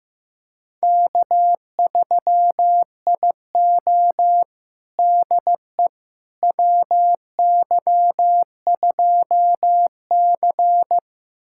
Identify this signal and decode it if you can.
Morse code, heard as a keyed tone, about 15 wpm, 700 Hz. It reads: K3IO DE WY2C
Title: USA Callsign DE Callsign - 4 Characters - 15wpm